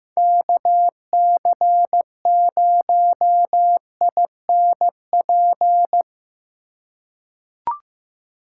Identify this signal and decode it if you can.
Morse code, heard as a keyed tone, about 15 wpm, 700 Hz. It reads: KC0INP
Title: K1USN SST QSO Practice - 15wpm